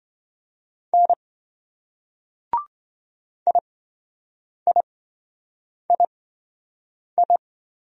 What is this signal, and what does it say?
Morse code, heard as a keyed tone, about 27 wpm, spaced as for 7 wpm, 700 Hz. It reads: N IIII